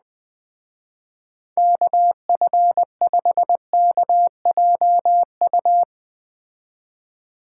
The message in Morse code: KF5KJU

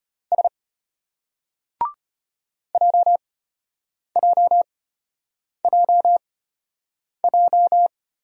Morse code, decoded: S JJJJ

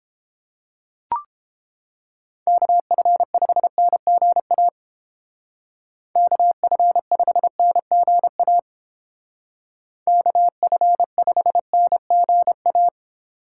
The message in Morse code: KF5NGA KF5NGA KF5NGA

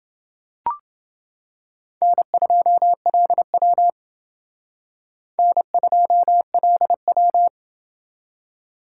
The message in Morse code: N2LW N2LW

